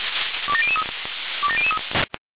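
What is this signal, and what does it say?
Five-tone selective call (ZVEI), heard as quick runs of five tones, tones of about 70 ms. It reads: 27003 27003
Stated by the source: ZVEI five-tone selective call 27002 received on SDR.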